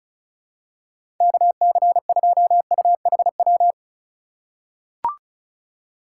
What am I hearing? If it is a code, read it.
Morse code, heard as a keyed tone, about 34 wpm, 700 Hz. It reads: KC2UHW
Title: USA Call Signs - 6 Characters - 2x Character Spacing - 35wpm